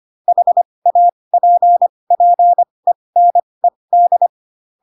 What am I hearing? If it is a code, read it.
Morse code, heard as a keyed tone, about 25 wpm, spaced as for 20 wpm, 700 Hz. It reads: HAPPENED
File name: Pinocchio / Pinocchio ch1 15 25 ik1hge